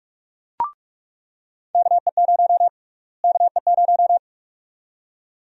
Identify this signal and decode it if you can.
Morse code, heard as a keyed tone, about 46 wpm, 700 Hz. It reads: KE0 KE0
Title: USA Call Sign Prefixes - Repeated 2x - 45wpm